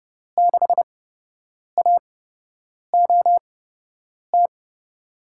Morse code, heard as a keyed tone, about 30 wpm, spaced as for 8 wpm, 700 Hz. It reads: 6AOT
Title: USA Call Signs - 5 Characters - 8x Character Spacing - 30wpm